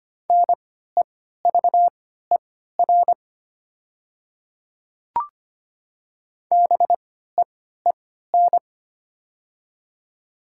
Morse code, heard as a keyed tone, about 25 wpm, spaced as for 14 wpm, 690 Hz. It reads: NEVER BEEN
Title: 3x Character Spacing - Top 300 Words - 25wpm